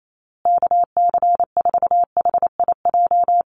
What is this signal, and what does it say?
Morse code, heard as a keyed tone, about 28 wpm, 700 Hz. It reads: KC4HIJ
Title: USA Call Signs Repeated 2x 28wpm